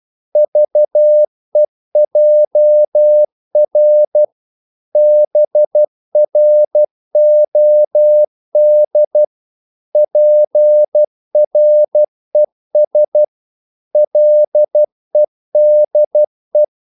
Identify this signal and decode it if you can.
Morse code, heard as a keyed tone, about 12 wpm, 600 Hz. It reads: VEJR BROD PRES LEDE
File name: Korte ord DK 12wpm